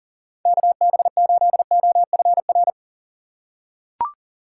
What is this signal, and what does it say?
Morse code, heard as a keyed tone, about 40 wpm, 700 Hz. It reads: KB8OFR
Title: USA Call Signs - 6x Character Spacing - 40wpm